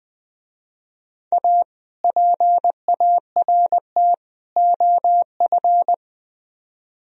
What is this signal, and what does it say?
Morse code, heard as a keyed tone, about 20 wpm, 700 Hz. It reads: A PART OF
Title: Speed-Racing - Sets of 3 Words 17wpm